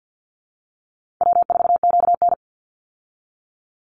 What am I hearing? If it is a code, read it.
Morse code, heard as a keyed tone, about 50 wpm, 700 Hz. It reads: W4QN
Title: Easy Call Signs Repeated 2x - 50wpm